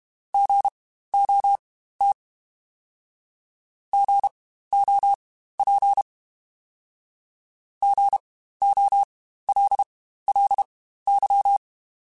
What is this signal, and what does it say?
Morse code, heard as a keyed tone, about 32 wpm, spaced as for 15 wpm, 800 Hz. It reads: GOT GOP GOLLY
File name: Spaced Alphas / Spaced Alphas mp3s / Gs sp @30